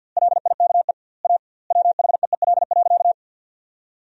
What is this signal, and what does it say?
Morse code, heard as a keyed tone, about 50 wpm, 700 Hz. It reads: LIKE A WHEEL.
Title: 2x Word-Spacing - Sentences from Top 500 Words - 50wpm